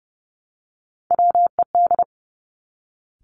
Morse code, heard as a keyed tone, about 30 wpm, 700 Hz. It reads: WED
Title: Three Challenging Letter-Number Combinations 30wpm